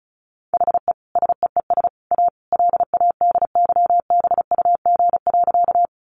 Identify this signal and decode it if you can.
Morse code, heard as a keyed tone, about 34 wpm, 700 Hz. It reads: HE SEES A LADYBUG.
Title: Additional Easy Phrases 35wpm